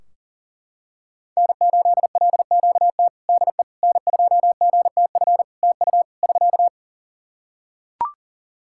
Morse code, heard as a keyed tone, about 40 wpm, 700 Hz. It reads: N8LQT DE N2GTF TU <SK>
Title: K1USN SST QSO Practice 40wpm